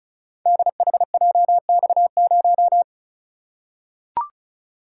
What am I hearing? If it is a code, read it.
Morse code, heard as a keyed tone, about 36 wpm, 700 Hz. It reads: DHJX0
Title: Code Groups - 5 Letter:Numbers - 35wpm